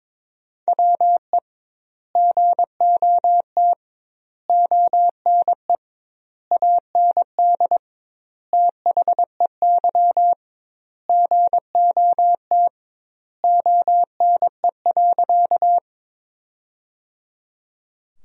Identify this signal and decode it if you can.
Morse code, heard as a keyed tone, about 22 wpm, 700 Hz. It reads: WE GOT ONE AND THEY GOT ONE.
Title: 2x Word-Spacing - Sentences from Top 300 Words - 22wpm